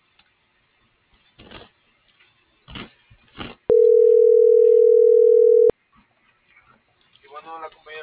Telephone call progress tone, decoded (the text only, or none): ringback tone